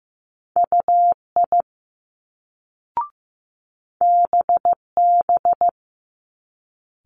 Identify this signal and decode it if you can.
Morse code, heard as a keyed tone, about 15 wpm, 700 Hz. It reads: UI BB
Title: Multi-letter Phonemes Quick Fire 15wpm